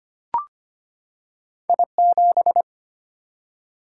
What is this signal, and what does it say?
Morse code, heard as a keyed tone, about 25 wpm, 700 Hz. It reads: I7
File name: Two Letter-Number Combinations - 25wpm